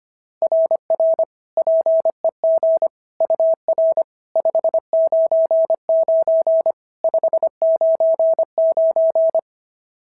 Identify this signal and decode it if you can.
Morse code, heard as a keyed tone, about 22 wpm, 650 Hz. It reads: RR PEG UR 599 599
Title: SOTA QSO Practice 25wpm